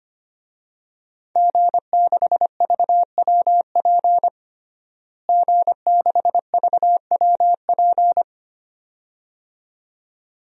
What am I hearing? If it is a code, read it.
Morse code, heard as a keyed tone, about 25 wpm, 700 Hz. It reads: G6VWP G6VWP
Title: DXCC-100 Call Signs - 5 Characters - Repeated 2x - 25wpm